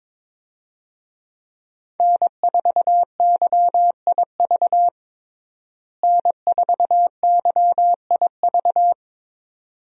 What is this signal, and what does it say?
Morse code, heard as a keyed tone, about 22 wpm, 700 Hz. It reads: N4YIV N4YIV
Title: USA Call Signs - 5 Characters - Repeated 2x - 22wpm